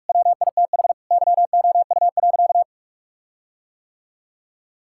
Morse code, heard as a keyed tone, about 45 wpm, 700 Hz. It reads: WITH YOU.
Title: Sentences from Top 1600 Words 45wpm